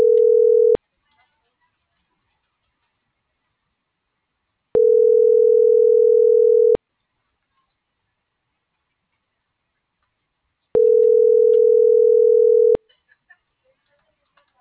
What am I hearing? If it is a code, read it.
Ringback tone (North America), 440 + 480 Hz.